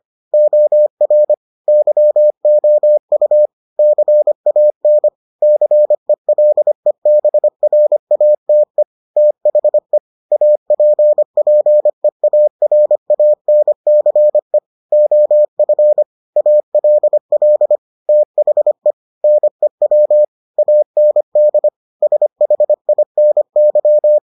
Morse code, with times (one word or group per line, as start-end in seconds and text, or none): 0.3-1.3 OR
1.7-3.5 YOU
3.8-5.1 CAN
5.4-8.8 CELEBRATE
9.2-10.0 THE
10.3-14.6 APPEARANCE
14.9-16.0 OF
16.4-17.8 ALL
18.1-18.9 THE
19.2-20.2 NEW
20.6-21.7 AND
22.0-24.2 SHINY